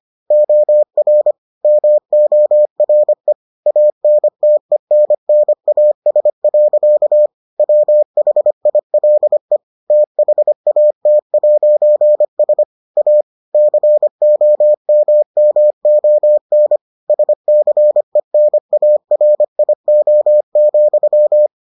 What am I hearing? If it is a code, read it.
Morse code, heard as a keyed tone, about 25 wpm, 600 Hz. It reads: OR MORE ANTENNAS. WHILE THAT'S A COMMON SCENARIO,